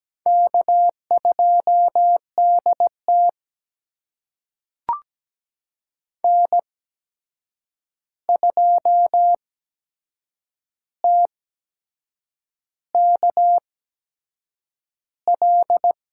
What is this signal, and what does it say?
Morse code, heard as a keyed tone, about 17 wpm, spaced as for 7 wpm, 700 Hz. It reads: K2DT N2TKL
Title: USA Call Signs - 8x Character Spacing - 17wpm